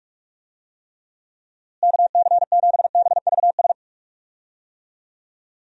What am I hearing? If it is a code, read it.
Morse code, heard as a keyed tone, about 44 wpm, 700 Hz. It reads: KC7BVS
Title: USA Call Signs - 6 Characters - Repeated 2x - 45wpm